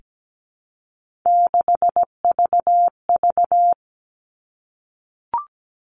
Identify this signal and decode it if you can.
Morse code, heard as a keyed tone, about 17 wpm, 700 Hz. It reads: 6VV
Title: All Cut Numbers - 3-Digit 17wpm